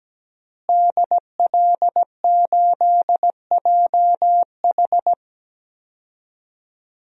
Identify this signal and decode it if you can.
Morse code, heard as a keyed tone, about 17 wpm, 700 Hz. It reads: DL8JH
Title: DXCC-100 Call Signs - 5 Characters - Repeated 3x - 17wpm